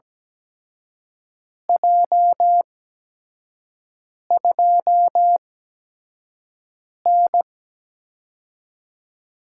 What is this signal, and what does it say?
Morse code, heard as a keyed tone, about 17 wpm, spaced as for 5 wpm, 700 Hz. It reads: J2N
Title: USA Call Signs - 4 Characters - 8x Character Spacing - 17wpm